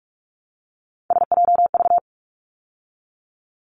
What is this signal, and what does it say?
Morse code, heard as a keyed tone, about 44 wpm, 700 Hz. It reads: SJV
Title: ARRL Sections 45wpm